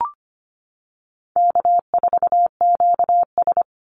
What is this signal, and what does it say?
Morse code, heard as a keyed tone, about 25 wpm, 700 Hz. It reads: K4QS
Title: NA SOTA Chaser Callsigns 25wpm